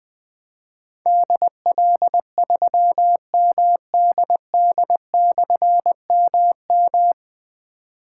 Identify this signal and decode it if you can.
Morse code, heard as a keyed tone, about 21 wpm, 700 Hz. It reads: DL3MDD/MM
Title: Difficult Call Signs 20wpm